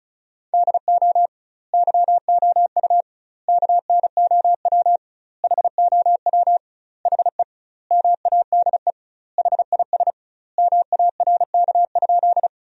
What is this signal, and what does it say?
Morse code, heard as a keyed tone, about 34 wpm, 700 Hz. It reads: DO YOU KNOW HOW HE MADE HIS MARK?
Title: 2x Word-Spacing - Sentences from Top 300 Words - 35wpm